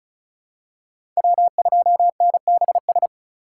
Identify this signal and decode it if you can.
Morse code, heard as a keyed tone, about 36 wpm, 700 Hz. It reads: W2NBS
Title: USA Call Signs - 4x Character Spacing - 35wpm